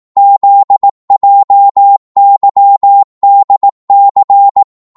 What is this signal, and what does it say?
Morse code, heard as a keyed tone, about 18 wpm, 800 Hz. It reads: ZJYDC
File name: Morse / ltr-18wpm